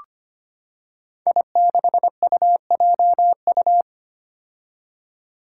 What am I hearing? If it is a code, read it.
Morse code, heard as a keyed tone, about 25 wpm, 700 Hz. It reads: I6UJU
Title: Five Challenging Letter-Number Combinations 25wpm